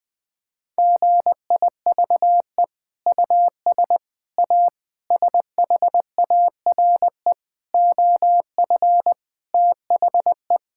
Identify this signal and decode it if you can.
Morse code, heard as a keyed tone, about 20 wpm, 700 Hz. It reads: GIVE US A SHARE OF THE
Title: Sentences from Top 1000 Words - Review Part 6 - 20wpm